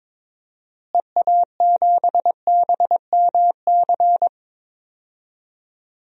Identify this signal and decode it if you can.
Morse code, heard as a keyed tone, about 22 wpm, 700 Hz. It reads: EA7BMC
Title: DXCC-100 Call Signs - 6 Characters - Repeated 2x - 22wpm